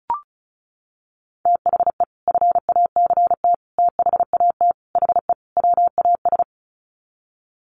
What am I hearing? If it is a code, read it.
Morse code, heard as a keyed tone, about 34 wpm, 700 Hz. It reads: THE FACT THAT HE WAS
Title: Sets of 5 Words 35wpm